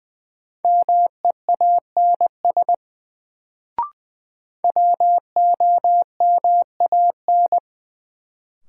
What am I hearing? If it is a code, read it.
Morse code, heard as a keyed tone, about 20 wpm, 700 Hz. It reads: MEANS WOMAN
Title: Sets of 5 Words - New Word Drill - 20wpm